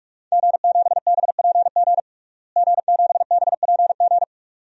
Morse code, heard as a keyed tone, about 46 wpm, 700 Hz. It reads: G7BPG G7BPG
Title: DXCC-100 Call Signs - 5 Characters - Repeated 2x - 45wpm